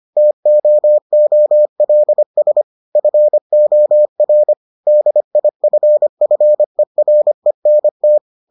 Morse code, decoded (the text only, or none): TOOLS FOR DIFFERENT